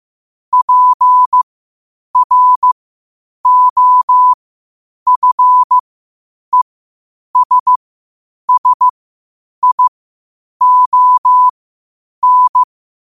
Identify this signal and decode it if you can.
Morse code, heard as a keyed tone, about 15 wpm, spaced as for 8 wpm, 1000 Hz. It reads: PROFESSION
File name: qotdfiles / 8wpm